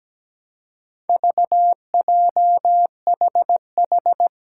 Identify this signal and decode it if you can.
Morse code, heard as a keyed tone, about 17 wpm, 700 Hz. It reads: VJHH
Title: Four Challenging Letter-Number Combinations 17wpm